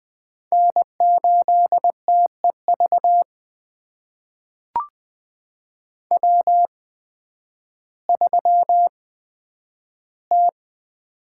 Morse code, heard as a keyed tone, about 20 wpm, spaced as for 11 wpm, 700 Hz. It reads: N8TEV W3T
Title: USA Call Signs - 5 Characters - 8x Character Spacing - 20wpm